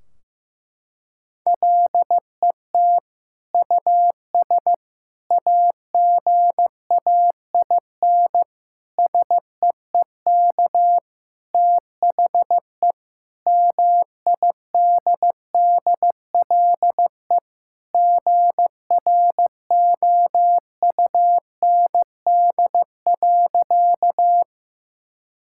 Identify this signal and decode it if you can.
Morse code, heard as a keyed tone, about 15 wpm, 700 Hz. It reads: LET US AGAIN SEEK THE MIDDLE GROUND.
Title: Sentences from Top 1600 Words 15wpm